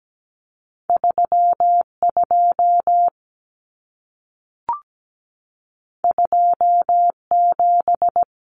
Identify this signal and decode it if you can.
Morse code, heard as a keyed tone, about 17 wpm, 700 Hz. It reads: 32 27
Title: Common Cut Numbers - 2-Digit 17wpm